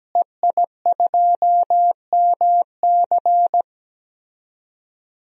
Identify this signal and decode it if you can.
Morse code, heard as a keyed tone, about 17 wpm, 700 Hz. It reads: EI2MC
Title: DXCC-100 Call Signs - 5 Characters - Repeated 2x - 17wpm